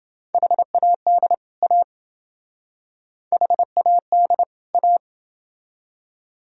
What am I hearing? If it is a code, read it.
Morse code, heard as a keyed tone, about 28 wpm, 700 Hz. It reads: HAD A HAD A